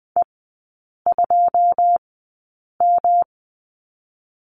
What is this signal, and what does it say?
Morse code, heard as a keyed tone, about 20 wpm, spaced as for 8 wpm, 700 Hz. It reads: E2M
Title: 166 Lesson 27e - Number 7 - Three Characters 20wpm